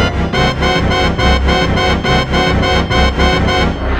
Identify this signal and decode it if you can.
There is no code.